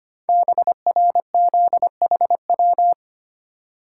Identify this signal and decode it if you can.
Morse code, heard as a keyed tone, about 25 wpm, 700 Hz. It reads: BRZHW
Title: Code Groups - 5 Letters - 25wpm